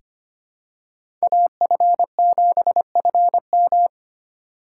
Morse code, heard as a keyed tone, about 25 wpm, 700 Hz. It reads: AF7FM
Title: USA Call Signs - 5 Characters - 3x Character Spacing - 25wpm